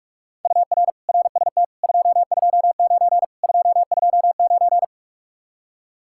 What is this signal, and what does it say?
Morse code, heard as a keyed tone, about 45 wpm, 700 Hz. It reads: UR RST 229 229